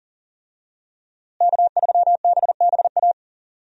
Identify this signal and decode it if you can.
Morse code, heard as a keyed tone, about 40 wpm, 700 Hz. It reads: K3BBA
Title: USA Call Signs - 5 Characters - Repeated 3x - 40wpm